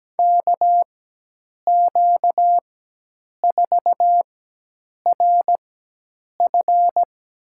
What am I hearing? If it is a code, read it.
Morse code, heard as a keyed tone, about 17 wpm, spaced as for 8 wpm, 700 Hz. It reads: KQ4RF
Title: USA Call Signs - 4x Character Spacing - 17wpm